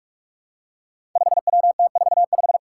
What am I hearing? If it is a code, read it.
Morse code, heard as a keyed tone, about 41 wpm, 700 Hz. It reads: 5WT45
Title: Five Challenging Letter-Number Combinations 45wpm